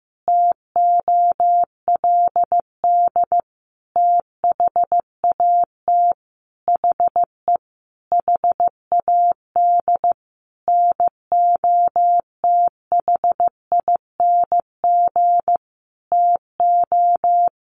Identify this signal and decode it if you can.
Morse code, heard as a keyed tone, about 15 wpm, 700 Hz. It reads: TOLD THAT HE HAD NOTHING TO
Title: Sentences from Top 1500 Words 15wpm